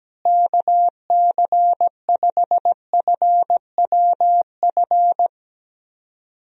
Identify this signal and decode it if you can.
Morse code, heard as a keyed tone, about 17 wpm, 700 Hz. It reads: KC5FWF
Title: USA Call Signs - 6 Characters - Repeated 2x - 17wpm